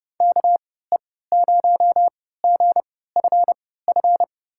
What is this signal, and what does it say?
Morse code, heard as a keyed tone, about 30 wpm, spaced as for 17 wpm, 700 Hz. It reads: KE0GFF
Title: USA Call Signs - 6 Characters - 3x Character Spacing - 30wpm